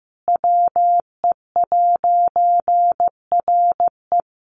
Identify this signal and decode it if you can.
Morse code, heard as a keyed tone, about 15 wpm, 700 Hz. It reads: WE'RE
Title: A Sound Of Thunder 15wpm